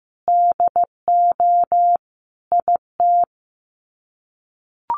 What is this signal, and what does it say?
Morse code, heard as a keyed tone, about 15 wpm, 700 Hz. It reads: DO IT
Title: Sets of 5 Words Encore2 15wpm